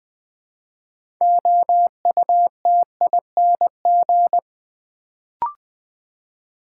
Morse code, heard as a keyed tone, about 20 wpm, 700 Hz. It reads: OUTING
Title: Build A Word - 3 Letters At A Time - 6-Letter Words - 20wpm